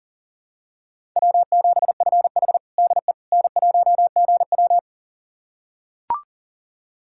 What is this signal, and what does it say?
Morse code, heard as a keyed tone, about 40 wpm, 700 Hz. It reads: W7FH DE N1GW
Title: USA Callsign DE Callsign - 4 Characters - 40wpm